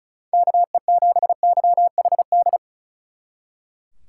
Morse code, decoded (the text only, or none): KE7YHD